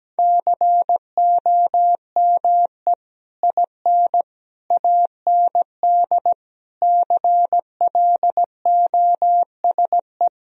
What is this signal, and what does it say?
Morse code, heard as a keyed tone, about 17 wpm, 700 Hz. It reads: COME IN AND CLOSE
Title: Sentences from Top 1000 Words - Review Part 1 - 17wpm